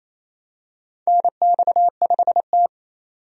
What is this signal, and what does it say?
Morse code, heard as a keyed tone, about 28 wpm, 690 Hz. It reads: NX5T
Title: USA Call Signs - 4 Characters - 28wpm